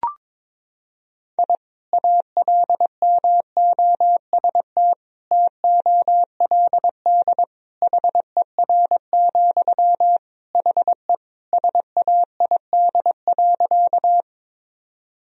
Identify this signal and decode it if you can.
Morse code, heard as a keyed tone, about 22 wpm, 700 Hz. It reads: I ALMOST TOLD HER, HE SAID.
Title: Sentences from Top 1000 Words - Review Part 4 - 22wpm